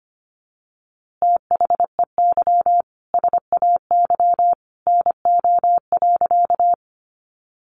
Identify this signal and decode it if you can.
Morse code, heard as a keyed tone, about 25 wpm, 700 Hz. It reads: THEY SAY NO.
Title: Easy Phrases 25wpm